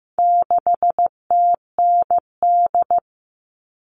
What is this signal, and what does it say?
Morse code, heard as a keyed tone, about 15 wpm, 700 Hz. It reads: 6TND
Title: All Cut Numbers - 4-Digit 15wpm